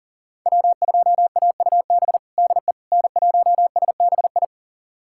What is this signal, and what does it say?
Morse code, heard as a keyed tone, about 40 wpm, 700 Hz. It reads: W2AUB DE N1SBI
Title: USA Callsign DE Callsign 40wpm